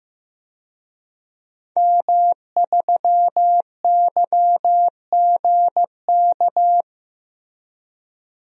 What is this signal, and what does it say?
Morse code, heard as a keyed tone, about 15 wpm, 700 Hz. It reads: M3YGK
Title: DXCC-100 Call Signs - 5 Characters - Repeated 2x - 15wpm